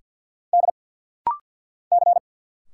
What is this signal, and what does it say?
Morse code, heard as a keyed tone, about 50 wpm, 700 Hz. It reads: D C
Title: Single Letters - Rapid-Fire 50wpm